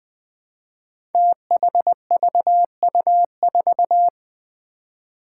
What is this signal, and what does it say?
Morse code, heard as a keyed tone, about 21 wpm, 700 Hz. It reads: THVU4
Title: Five Challenging Letter-Number Combinations 20wpm